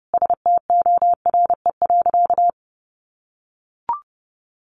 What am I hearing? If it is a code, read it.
Morse code, heard as a keyed tone, about 30 wpm, 700 Hz. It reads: STORE.
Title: Easy Phrases 30wpm